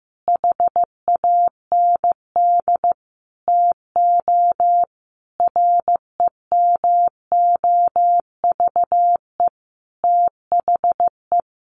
Morse code, heard as a keyed tone, about 15 wpm, 700 Hz. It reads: HAND TO REMOVE THE
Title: Sentences from Top 800 Words 15wpm